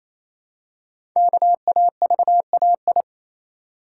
Morse code, heard as a keyed tone, about 29 wpm, 700 Hz. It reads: KAVAI
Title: Code Groups - 5 Letters - 28wpm